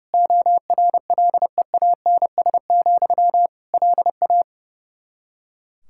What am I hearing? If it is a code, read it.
Morse code, heard as a keyed tone, about 32 wpm, 700 Hz. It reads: ORLEANS, LA
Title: K1USN SST QSO Practice 30wpm